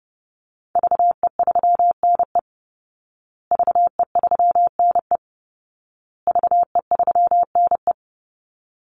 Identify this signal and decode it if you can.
Morse code, heard as a keyed tone, about 28 wpm, 700 Hz. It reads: VE3NE VE3NE VE3NE
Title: Call Signs Repeated 2x - 30wpm